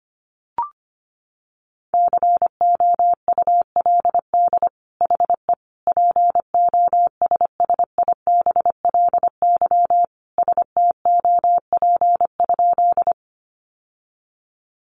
Morse code, heard as a keyed tone, about 25 wpm, 700 Hz. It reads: COULD HE POSSIBLY STOP?
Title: Sentences from Top 1300 Words 25wpm